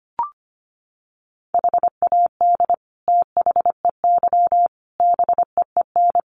Morse code, heard as a keyed tone, about 25 wpm, 700 Hz. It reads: HAD THEY BEEN
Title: Sentences from Top 400 Words 25wpm